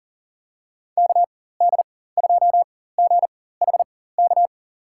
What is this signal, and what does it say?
Morse code, heard as a keyed tone, about 37 wpm, spaced as for 18 wpm, 700 Hz. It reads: KD2GHK
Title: USA Call Signs - 6 Characters - 4x Character Spacing - 40wpm